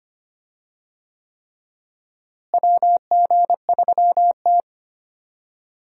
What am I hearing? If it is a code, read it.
Morse code, heard as a keyed tone, about 25 wpm, 700 Hz. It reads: WG3T
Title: USA Call Signs - 4 Characters - Repeated 3x - 25wpm